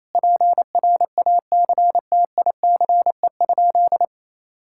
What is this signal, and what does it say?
Morse code, heard as a keyed tone, about 28 wpm, 700 Hz. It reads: PRACTICE?
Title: Sentences from Top 1000 Words - Review Part 6 - 28wpm